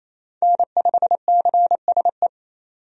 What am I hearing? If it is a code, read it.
Morse code, heard as a keyed tone, about 28 wpm, 700 Hz. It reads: N5CSE
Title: USA Call Signs - 5 Characters - 28wpm